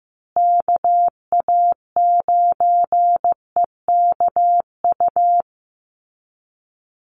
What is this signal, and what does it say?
Morse code, heard as a keyed tone, about 15 wpm, 700 Hz. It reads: KA9EKU
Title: USA Call Signs Repeated 2x 15wpm